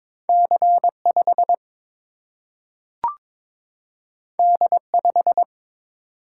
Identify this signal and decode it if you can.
Morse code, heard as a keyed tone, about 21 wpm, 700 Hz. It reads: C 5 D 5
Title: Single-Letter Single-Number 22wpm